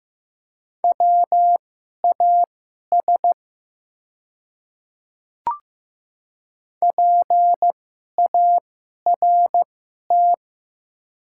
Morse code, heard as a keyed tone, about 15 wpm, spaced as for 11 wpm, 700 Hz. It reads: WAS PART
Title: 2x Character Spacing - Top 500 Words - 15wpm